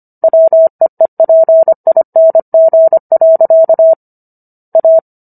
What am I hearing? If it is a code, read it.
Morse code, heard as a keyed tone, about 25 wpm, 650 Hz. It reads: WEEPING. A